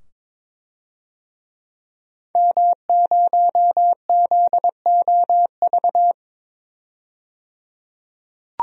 Morse code, heard as a keyed tone, about 22 wpm, 700 Hz. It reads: M0ZOV
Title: DXCC-100 Call Signs - 5 Characters - 22wpm